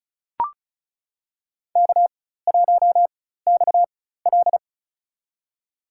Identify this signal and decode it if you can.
Morse code, heard as a keyed tone, about 34 wpm, spaced as for 16 wpm, 700 Hz. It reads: K1XL
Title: USA Call Signs - 4 Characters - 4x Character Spacing - 35wpm